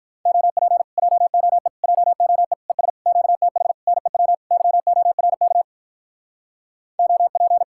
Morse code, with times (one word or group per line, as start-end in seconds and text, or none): 0.3-0.8 OP
1.0-1.7 JOE
1.8-2.5 JOE
2.7-2.9 ES
3.1-3.7 QTH
3.9-4.3 NEW
4.5-5.6 YORK
7.0-7.6 OP